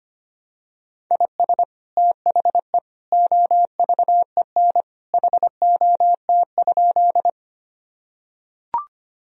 Morse code, heard as a keyed tone, about 25 wpm, 700 Hz. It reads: IS THE OVEN HOT?
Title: Sentences from Top 1000 Words - Review Part 4 - 25wpm